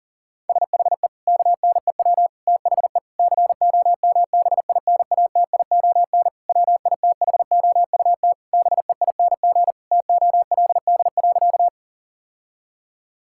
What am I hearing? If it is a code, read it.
Morse code, heard as a keyed tone, about 40 wpm, 700 Hz. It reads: SHE KNEW THE COMBINATION WITHOUT BEING TOLD.
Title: Sentences from Top 1000 Words - Review Part 6 - 40wpm